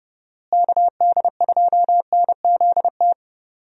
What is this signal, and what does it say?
Morse code, heard as a keyed tone, about 30 wpm, 700 Hz. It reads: KD2NZT
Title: USA Call Signs - 6 Characters - Repeated 2x - 30wpm